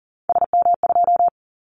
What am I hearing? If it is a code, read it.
Morse code, heard as a keyed tone, about 40 wpm, 700 Hz. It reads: SM2
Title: Callsign Prefixes for Top 25 DXCC - Repeated 3x - 40wpm